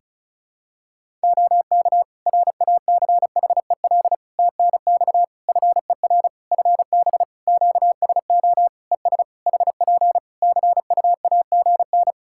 Morse code, heard as a keyed tone, about 36 wpm, 700 Hz. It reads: OK RACHEL TNX FER FB QSO ES HP CUAGN
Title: K1USN SST QSO Practice 35wpm